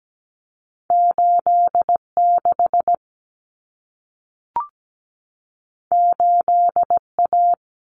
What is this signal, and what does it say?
Morse code, heard as a keyed tone, about 17 wpm, 700 Hz. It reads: 86 8A
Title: Common Cut Numbers - 2-Digit 17wpm